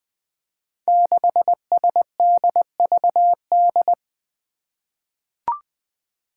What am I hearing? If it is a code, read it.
Morse code, heard as a keyed tone, about 20 wpm, 700 Hz. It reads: 6SDVD
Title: Five Challenging Letter-Number Combinations 20wpm